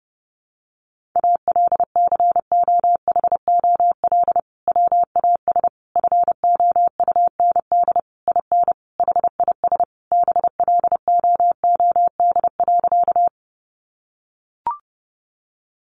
Morse code, heard as a keyed tone, about 30 wpm, 700 Hz. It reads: ALCOHOL WAS FOUND IN HIS BLOOD.
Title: Sentences from Top 1400 Words 30wpm